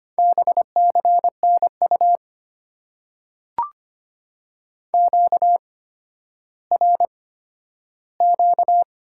Morse code, heard as a keyed tone, about 25 wpm, spaced as for 12 wpm, 700 Hz. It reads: BCNU QRQ
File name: Top 90 QSO Elements - 8x Character Spacing - 25wpm